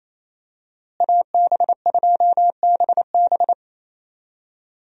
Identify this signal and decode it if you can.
Morse code, heard as a keyed tone, about 28 wpm, 700 Hz. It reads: AB2BB
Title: USA Call Signs - 5 Characters - Repeated 3x - 28wpm